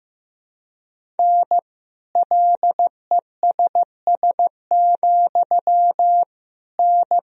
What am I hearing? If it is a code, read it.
Morse code, heard as a keyed tone, about 15 wpm, 700 Hz. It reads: N LESS, N
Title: Sentences from Top 400 Words and Abbreviations 15wpm